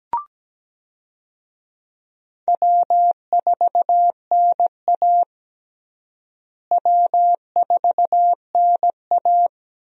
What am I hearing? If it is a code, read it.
Morse code, heard as a keyed tone, about 17 wpm, 700 Hz. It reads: W4NA W4NA
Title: USA Call Signs - 4 Characters - Repeated 2x - 17wpm